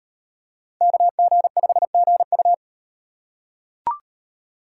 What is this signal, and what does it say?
Morse code, heard as a keyed tone, about 38 wpm, 700 Hz. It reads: KG5GU